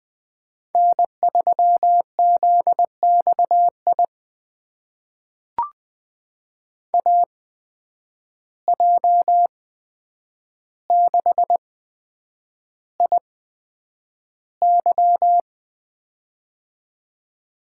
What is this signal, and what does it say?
Morse code, heard as a keyed tone, about 20 wpm, spaced as for 9 wpm, 700 Hz. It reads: N3ZXI AJ6IY
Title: USA Call Signs - 5 Characters - 8x Character Spacing - 20wpm